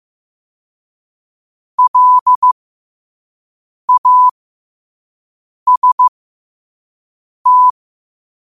Morse code, heard as a keyed tone, about 15 wpm, spaced as for 5 wpm, 1000 Hz. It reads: LAST
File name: qotdfiles / 5wpm